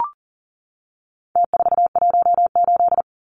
Morse code, heard as a keyed tone, about 40 wpm, 700 Hz. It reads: T418
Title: Numbers - Contest Style - 4-Digit 40wpm